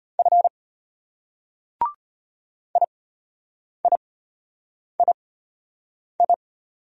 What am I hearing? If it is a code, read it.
Morse code, heard as a keyed tone, about 33 wpm, spaced as for 8 wpm, 700 Hz. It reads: F IIII